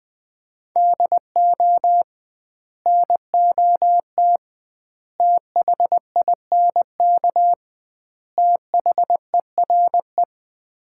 Morse code, heard as a keyed tone, about 20 wpm, 700 Hz. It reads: DO NOT THINK THERE
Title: 2x Word-Spacing - Sets of 4 Words - 20wpm